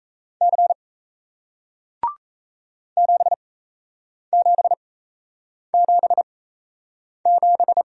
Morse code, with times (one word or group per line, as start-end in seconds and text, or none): 0.4-0.7 C
3.0-7.8 7777